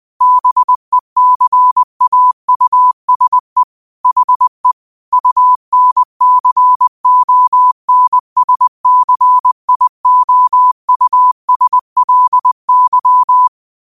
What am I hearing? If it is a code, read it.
Morse code, heard as a keyed tone, about 20 wpm, 1000 Hz. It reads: BECAUSE HE UNCONSCIOUSLY
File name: qotdfiles / 20wpm